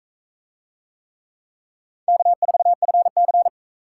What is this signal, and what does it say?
Morse code, heard as a keyed tone, about 41 wpm, 700 Hz. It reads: K4FC